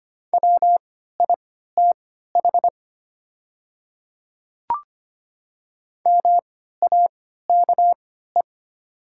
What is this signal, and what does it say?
Morse code, heard as a keyed tone, about 25 wpm, spaced as for 14 wpm, 700 Hz. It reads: WITH MAKE
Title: 3x Character Spacing - Top 100 Words - 25wpm